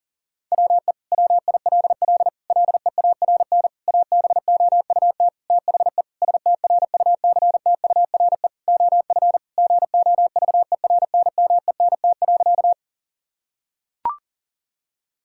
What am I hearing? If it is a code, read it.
Morse code, heard as a keyed tone, about 40 wpm, 700 Hz. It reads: WE WILL LEARN ABOUT THE STRUCTURE OF GOVERNMENT.
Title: Sentences from Top 1000 Words - Review Part 6 - 40wpm